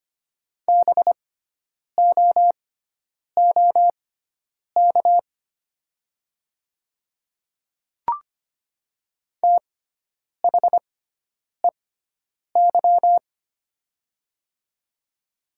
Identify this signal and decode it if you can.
Morse code, heard as a keyed tone, about 25 wpm, spaced as for 9 wpm, 700 Hz. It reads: BOOK THEY
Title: 6x Character Spacing - Top 300 Words - 25wpm